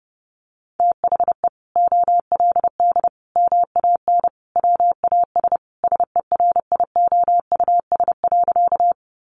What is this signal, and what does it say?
Morse code, heard as a keyed tone, about 30 wpm, 700 Hz. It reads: THE OLD MAN WAS SERIOUS.
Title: Sentences from Top 1100 Words 30wpm